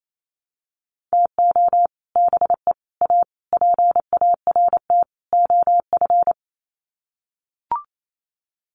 Morse code, heard as a keyed tone, about 28 wpm, 700 Hz. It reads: TO BE A PART OF
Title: Sets of 5 Words 28wpm